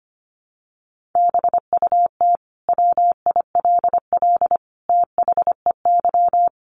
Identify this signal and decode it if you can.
Morse code, heard as a keyed tone, about 25 wpm, 700 Hz. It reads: BUT WILL THEY
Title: Sentences from Top 1000 Words 25wpm